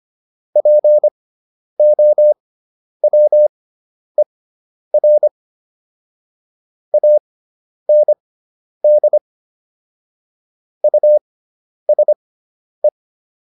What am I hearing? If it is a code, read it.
Morse code, heard as a keyed tone, about 25 wpm, spaced as for 10 wpm, 600 Hz. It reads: POWER AND USE